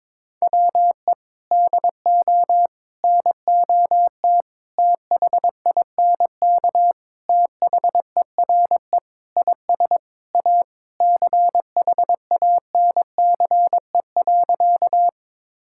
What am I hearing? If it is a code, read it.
Morse code, heard as a keyed tone, about 22 wpm, 700 Hz. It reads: WE DO NOT THINK THERE IS A CHANCE.
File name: practice / 22wpm / Sentences with 4-Word Phrases - 22wpm